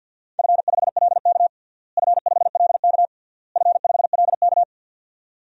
Morse code, heard as a keyed tone, about 52 wpm, 700 Hz. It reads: F5LK F5LK F5LK
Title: DXCC-100 Call Signs - 4 Characters - Repeated 3x - 50wpm